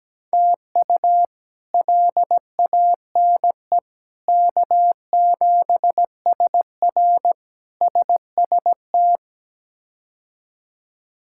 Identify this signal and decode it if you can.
Morse code, heard as a keyed tone, about 17 wpm, 700 Hz. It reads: TU LANE K7SR SST
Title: K1USN SST QSO Practice - 17wpm